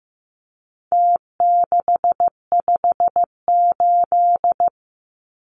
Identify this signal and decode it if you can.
Morse code, heard as a keyed tone, about 15 wpm, 700 Hz. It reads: T658
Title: Numbers - Contest Style - 4-Digit 15wpm